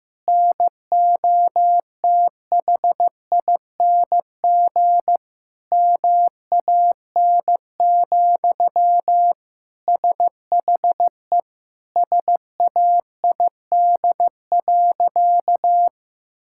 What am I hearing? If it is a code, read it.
Morse code, heard as a keyed tone, about 16 wpm, 700 Hz. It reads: NOTHING MAN, SHE SAID.
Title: Sentences from Top 1000 Words - Review Part 6 - 15wpm